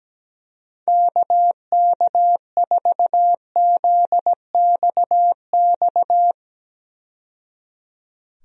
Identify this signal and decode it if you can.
Morse code, heard as a keyed tone, about 17 wpm, 700 Hz. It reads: KK4ZXX
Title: USA Call Signs - 6 Characters - 17wpm